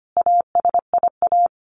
Morse code, heard as a keyed tone, about 25 wpm, 700 Hz. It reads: ASIA
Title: A Sound Of Thunder 25wpm